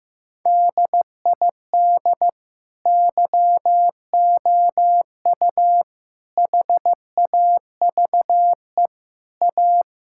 Morse code, heard as a keyed tone, about 15 wpm, 700 Hz. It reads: DID YOU HAVE A
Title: Sentences from Top 1600 Words 15wpm